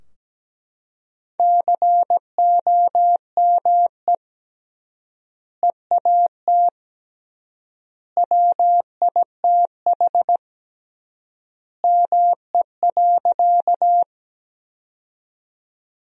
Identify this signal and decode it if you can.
Morse code, heard as a keyed tone, about 17 wpm, 700 Hz. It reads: COME EAT WITH ME.
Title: 3x Word-Spacing - Sentences from Top 300 Words - 17wpm